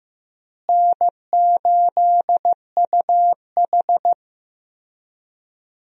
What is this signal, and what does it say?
Morse code, heard as a keyed tone, about 15 wpm, 700 Hz. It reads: N8UH
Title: USA Call Signs - 4 Characters - Repeated 2x - 15wpm